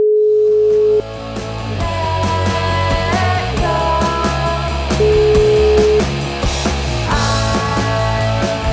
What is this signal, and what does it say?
Ringback tone (Europe, 425 Hz), with noise.